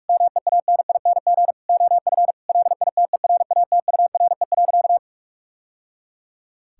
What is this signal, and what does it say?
Morse code, heard as a keyed tone, about 46 wpm, 700 Hz. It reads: MEANING OF LITERATURE.
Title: Sentences from Top 1000 Words - Review Part 6 - 45wpm